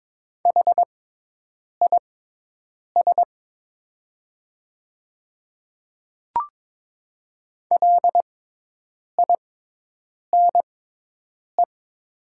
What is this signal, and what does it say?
Morse code, heard as a keyed tone, about 22 wpm, spaced as for 8 wpm, 700 Hz. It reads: HIS LINE
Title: 6x Character Spacing - Top 200 Words - 22wpm